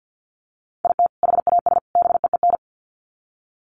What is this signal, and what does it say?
Morse code, heard as a keyed tone, about 48 wpm, 700 Hz. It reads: IT HAS BEEN
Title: Sets of 3 Words 50wpm